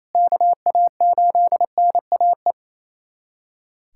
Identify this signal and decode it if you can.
Morse code, heard as a keyed tone, about 28 wpm, 700 Hz. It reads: KA8NAE
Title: USA Callsign DE Callsign - 6 Characters - 28wpm